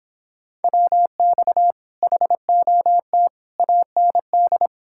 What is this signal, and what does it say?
Morse code, heard as a keyed tone, about 29 wpm, 700 Hz. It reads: WX HOT AND